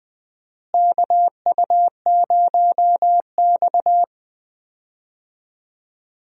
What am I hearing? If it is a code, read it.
Morse code, heard as a keyed tone, about 20 wpm, 700 Hz. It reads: KU0X
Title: USA Call Signs - 4 Characters - Repeated 2x - 20wpm